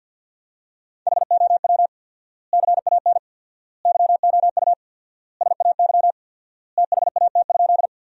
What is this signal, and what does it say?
Morse code, heard as a keyed tone, about 49 wpm, 700 Hz. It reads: HOW CAN YOU SAY THAT?
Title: 4x Word-Spacing - Sentences from Top 200 Words - 50wpm